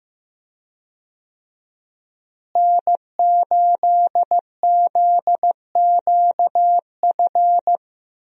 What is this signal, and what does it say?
Morse code, heard as a keyed tone, about 15 wpm, 700 Hz. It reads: N8ZQF
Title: USA Call Signs - 5 Characters - Repeated 3x - 15wpm